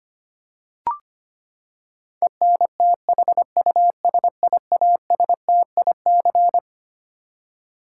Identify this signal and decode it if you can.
Morse code, heard as a keyed tone, about 25 wpm, 700 Hz. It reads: ENTHUSIASTIC
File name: Twelve Letter Words - 25wpm